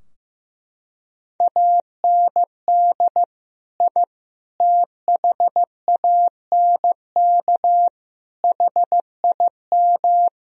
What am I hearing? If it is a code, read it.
Morse code, heard as a keyed tone, about 15 wpm, 700 Hz. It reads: AND I THANK HIM
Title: Sentences from Top 1000 Words - Review Part 3 - 15wpm